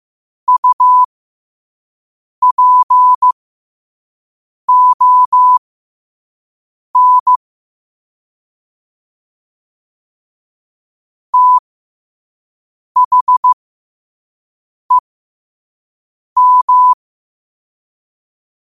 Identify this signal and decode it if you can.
Morse code, heard as a keyed tone, about 15 wpm, spaced as for 5 wpm, 1000 Hz. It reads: UPON THEM
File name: qotdfiles / 5wpm